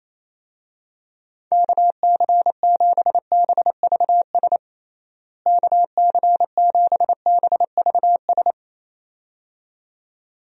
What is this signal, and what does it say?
Morse code, heard as a keyed tone, about 28 wpm, 700 Hz. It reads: KC7BVS KC7BVS
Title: USA Call Signs - 6 Characters - Repeated 2x - 28wpm